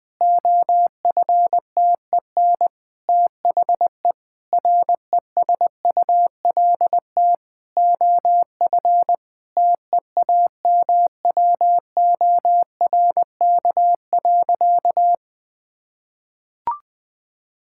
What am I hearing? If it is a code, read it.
Morse code, heard as a keyed tone, about 20 wpm, 700 Hz. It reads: OFTEN THE RESULT OF TEAMWORK.
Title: Difficult Sentences with Hints - 20wpm